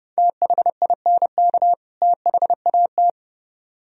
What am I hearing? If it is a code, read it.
Morse code, heard as a keyed tone, about 28 wpm, 700 Hz. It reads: THINK THAT